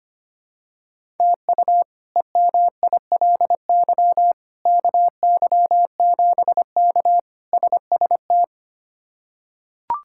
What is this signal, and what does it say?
Morse code, heard as a keyed tone, about 25 wpm, 700 Hz. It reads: TU EMILY KY7K SST
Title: K1USN SST QSO Practice - 25wpm